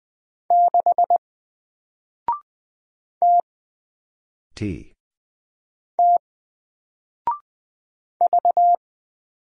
Morse code, heard as a keyed tone, about 20 wpm, 700 Hz. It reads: 6 T T V
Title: Single Character (Letter:Number:Punctuation:Prosign) 20wpm